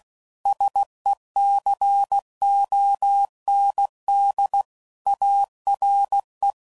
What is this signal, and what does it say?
Morse code, heard as a keyed tone, about 16 wpm, 800 Hz. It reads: SECOND ARE
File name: CW Code Groups @15WPM / Technical Subjects / XC